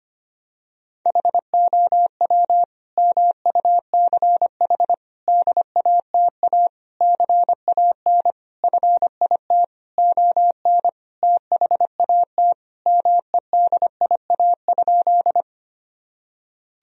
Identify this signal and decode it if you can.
Morse code, heard as a keyed tone, about 25 wpm, 700 Hz. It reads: HOW MUCH DATA CAN FIT ON THAT MEDIA?
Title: Sentences from Top 1000 Words - Review Part 4 - 25wpm